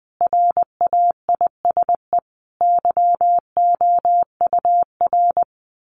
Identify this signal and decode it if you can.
Morse code, heard as a keyed tone, about 19 wpm, 700 Hz. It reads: RAISE YOUR
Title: Sentences from Top 1400 Words 20wpm